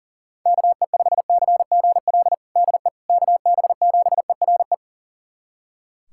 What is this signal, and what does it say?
Morse code, heard as a keyed tone, about 40 wpm, 700 Hz. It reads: KE5CGL DE KB7ERE
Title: USA Callsign DE Callsign - 6 Characters - 40wpm